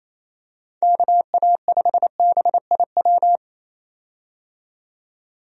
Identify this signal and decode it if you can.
Morse code, heard as a keyed tone, about 28 wpm, 700 Hz. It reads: KA5BIW
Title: USA Call Signs - 6 Characters - 28wpm